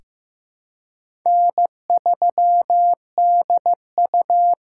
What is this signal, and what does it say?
Morse code, heard as a keyed tone, about 15 wpm, 700 Hz. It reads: N3DU
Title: USA Call Signs - 4x Character Spacing - 15wpm